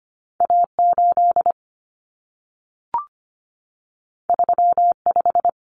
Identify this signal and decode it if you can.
Morse code, heard as a keyed tone, about 25 wpm, 700 Hz. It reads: A8 35
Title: Common Cut Numbers - 2-Digit 25wpm